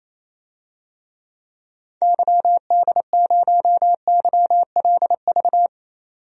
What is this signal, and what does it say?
Morse code, heard as a keyed tone, about 28 wpm, 700 Hz. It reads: YD0YLV
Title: DXCC-100 Call Signs - 6 Characters - 28wpm